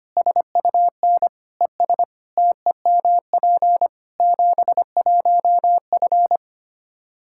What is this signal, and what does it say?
Morse code, heard as a keyed tone, about 25 wpm, 700 Hz. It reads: SUN ES TEMP 71F
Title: Practice QSO - Exchange 2 - Info - 25wpm